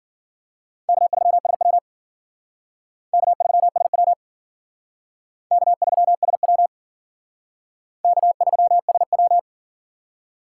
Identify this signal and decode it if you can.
Morse code, heard as a keyed tone, about 52 wpm, 700 Hz. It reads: K3SW K3SW K3SW K3SW